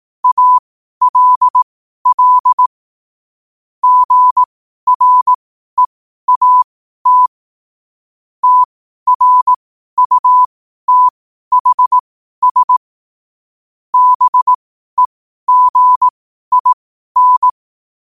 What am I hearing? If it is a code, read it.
Morse code, heard as a keyed tone, about 18 wpm, spaced as for 12 wpm, 1000 Hz. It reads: ALL GREAT TRUTHS BEGIN
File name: qotdfiles / 12wpm